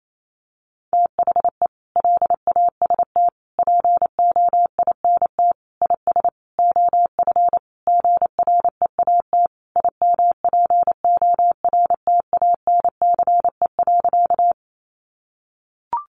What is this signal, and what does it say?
Morse code, heard as a keyed tone, about 28 wpm, 700 Hz. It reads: THE LAST POINT IS OF GREAT IMPORTANCE.
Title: Sentences from Top 1200 Words 28wpm